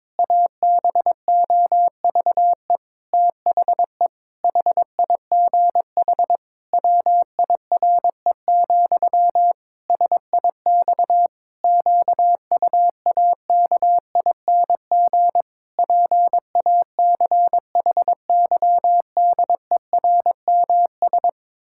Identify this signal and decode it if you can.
Morse code, heard as a keyed tone, about 21 wpm, 700 Hz. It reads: ABOVE THE HIGH WIRE, SIX QUAKING PACHYDERMS
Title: Pangram Challenge v2 - 22wpm